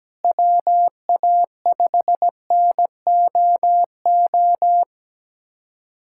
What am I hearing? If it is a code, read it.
Morse code, heard as a keyed tone, about 17 wpm, 700 Hz. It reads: WA5NOO
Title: USA Callsign DE Callsign - 6 Characters - 17wpm